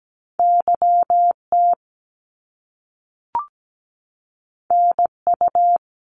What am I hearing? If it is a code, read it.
Morse code, heard as a keyed tone, about 17 wpm, 700 Hz. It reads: YT NU
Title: Canadian Provinces and Territories 17wpm